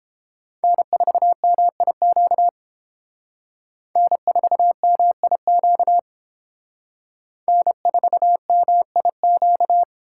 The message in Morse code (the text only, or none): N4MIQ N4MIQ N4MIQ